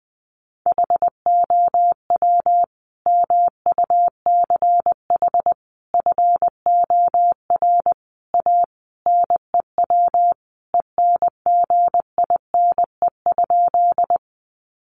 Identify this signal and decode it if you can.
Morse code, heard as a keyed tone, about 20 wpm, 700 Hz. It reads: HOW MUCH FOR A NEW ENGINE?
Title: Sentences from Top 1500 Words 20wpm